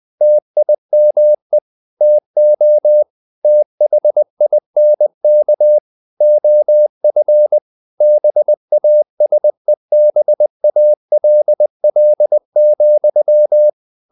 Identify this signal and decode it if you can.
Morse code, heard as a keyed tone, about 20 wpm, 600 Hz. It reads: TIME TO THINK OF BASEBALL,
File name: the old man and the sea / the old man and the sea part 1 20wpm